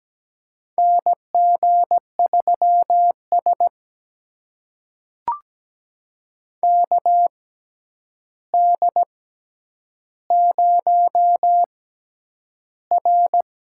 Morse code, heard as a keyed tone, about 17 wpm, spaced as for 9 wpm, 700 Hz. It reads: NG3S KD0R
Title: USA Call Signs - 6x Character Spacing - 17wpm